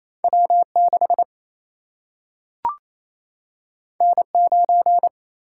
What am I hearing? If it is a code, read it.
Morse code, heard as a keyed tone, about 28 wpm, 700 Hz. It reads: W 6 N 9
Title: Single-Letter Single-Number 28wpm